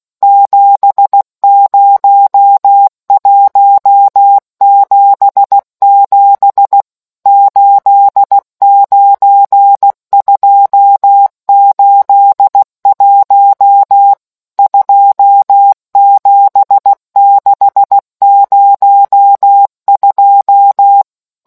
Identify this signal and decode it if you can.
Morse code, heard as a keyed tone, about 16 wpm, 800 Hz. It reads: 70177 89281 27602